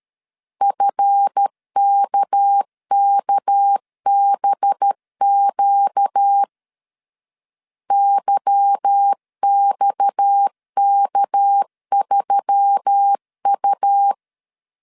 Morse code, heard as a keyed tone, about 13 wpm, 800 Hz. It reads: FKKBQ YXK3U
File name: CW / DET / 1 9 1500C S C 12wpm 3811